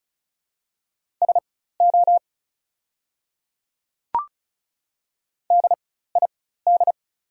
Morse code, heard as a keyed tone, about 36 wpm, spaced as for 16 wpm, 700 Hz. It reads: SO DID
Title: 4x Character Spacing - Top 300 Words - 35wpm